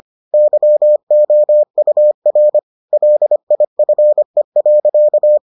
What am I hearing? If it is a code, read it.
Morse code, heard as a keyed tone, about 25 wpm, 600 Hz. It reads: YOUR LIFE.